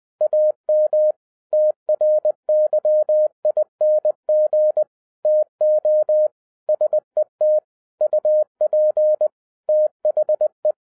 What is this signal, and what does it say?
Morse code, heard as a keyed tone, about 20 wpm, 610 Hz. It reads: AM TRYING TO SET UP THE